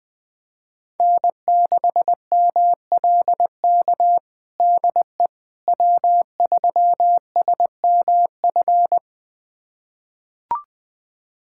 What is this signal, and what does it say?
Morse code, heard as a keyed tone, about 20 wpm, 700 Hz. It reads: N6MLK DE W3SMF
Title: USA Callsign DE Callsign - 5 Characters - 20wpm